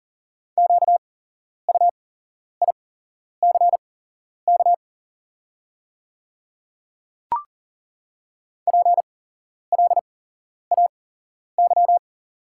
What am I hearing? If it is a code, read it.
Morse code, heard as a keyed tone, about 40 wpm, spaced as for 11 wpm, 690 Hz. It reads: QUICK PLAY
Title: 8x Character Spacing - Top 500 Words - 40wpm